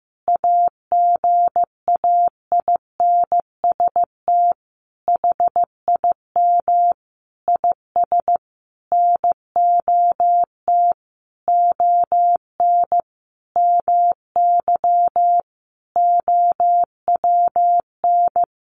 Morse code, heard as a keyed tone, about 15 wpm, 700 Hz. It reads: AGAINST HIM IS NOT ON MY OWN
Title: Sentences from Top 1400 Words 15wpm